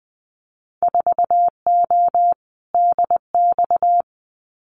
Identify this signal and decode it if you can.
Morse code, heard as a keyed tone, about 19 wpm, 700 Hz. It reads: 4O DX
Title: Winter Field Day Exchanges Encore 20wpm